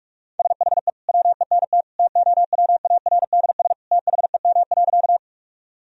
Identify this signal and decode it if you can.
Morse code, heard as a keyed tone, about 46 wpm, 700 Hz. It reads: SHE WENT TOWARDS THEM.
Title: Sentences from Top 1000 Words - Review Part 2 - 45wpm